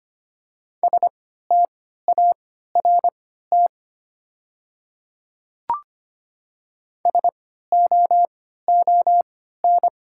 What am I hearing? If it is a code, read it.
Morse code, heard as a keyed tone, about 25 wpm, spaced as for 14 wpm, 700 Hz. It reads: START SOON
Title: 3x Character Spacing - Top 500 Words - 25wpm